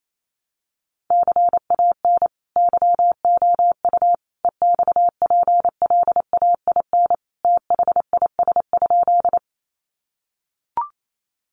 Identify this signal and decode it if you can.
Morse code, heard as a keyed tone, about 28 wpm, 700 Hz. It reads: CAN YOU EXPLAIN THIS?
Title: Sentences from Top 1100 Words 28wpm